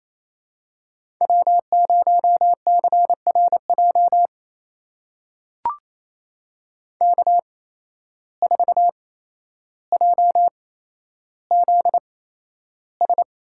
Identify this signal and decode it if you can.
Morse code, heard as a keyed tone, about 27 wpm, spaced as for 12 wpm, 700 Hz. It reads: W0CRJ K4JZS
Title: USA Call Signs - 5 Characters - 8x Character Spacing - 28wpm